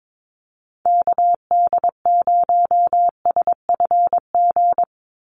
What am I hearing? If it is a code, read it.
Morse code, heard as a keyed tone, about 22 wpm, 700 Hz. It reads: KD0SFG
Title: USA Call Signs 22wpm